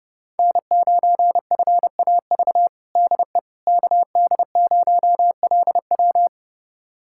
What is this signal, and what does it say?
Morse code, heard as a keyed tone, about 32 wpm, 700 Hz. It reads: N9FAV DE KD0LW
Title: USA Callsign DE Callsign - 5 Characters - 30wpm